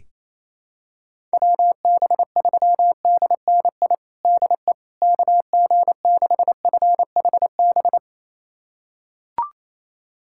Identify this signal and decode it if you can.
Morse code, heard as a keyed tone, about 28 wpm, 700 Hz. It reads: WB3DNI DE KG6FHB
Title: USA Callsign DE Callsign - 6 Characters - 28wpm